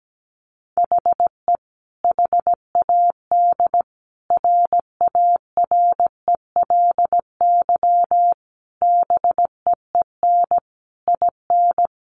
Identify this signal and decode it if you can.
Morse code, heard as a keyed tone, about 17 wpm, 700 Hz. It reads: HE HAD RARELY BEEN IN
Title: Sentences from Top 1400 Words 17wpm